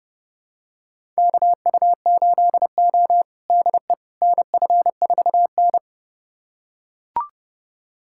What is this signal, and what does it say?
Morse code, heard as a keyed tone, about 30 wpm, 700 Hz. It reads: KU8O DE NF4N
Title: USA Callsign DE Callsign - 4 Characters - 30wpm